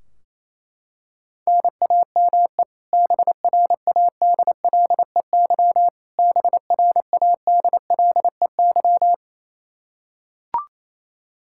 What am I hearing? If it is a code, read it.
Morse code, heard as a keyed tone, about 28 wpm, 700 Hz. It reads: NAME BRADLEY BRADLEY
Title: Practice QSO - Exchange 1 - Beginning - 28wpm